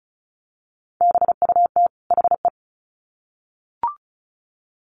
Morse code, heard as a keyed tone, about 34 wpm, 700 Hz. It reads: BUT HE
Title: Sets of 2 Words 35wpm